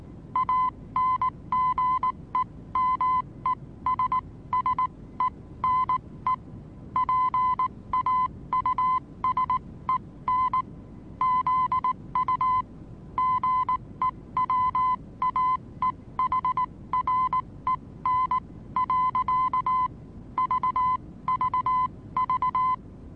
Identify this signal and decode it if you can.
Morse code, heard as a keyed tone, about 19 wpm, spaced as for 16 wpm, 1020 Hz. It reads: ANGEMESSENE PAUSEN ZU GEWAEHREN. VVV
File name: repos / Morsekurs / DJ4UF Morsen26